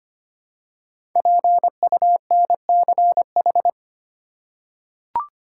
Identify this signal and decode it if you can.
Morse code, heard as a keyed tone, about 25 wpm, 700 Hz. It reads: PUNCH
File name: 5 Letter Words - First 3 Letters Given - 25wpm